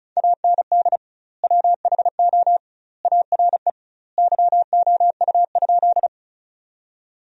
Morse code, heard as a keyed tone, about 36 wpm, 700 Hz. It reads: AND WHO ARE YOU?
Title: 2x Word-Spacing - Sentences from Top 100 Words - 35wpm